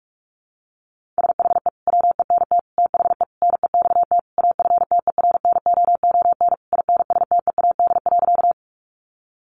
Morse code, heard as a keyed tone, about 44 wpm, 700 Hz. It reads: SHE WENT THE NEXT AFTERNOON INSTEAD.
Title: Sentences from Top 1500 Words 45wpm